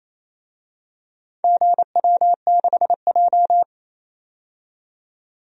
Morse code, heard as a keyed tone, about 28 wpm, 700 Hz. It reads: GW6J
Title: DXCC-100 Call Signs - 4 Characters - 28wpm